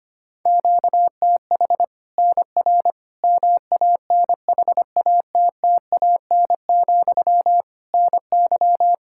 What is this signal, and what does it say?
Morse code, heard as a keyed tone, about 25 wpm, 700 Hz. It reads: QTH NR MANHATTAN, NY
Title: Practice QSO - Exchange 1 - Beginning - 25wpm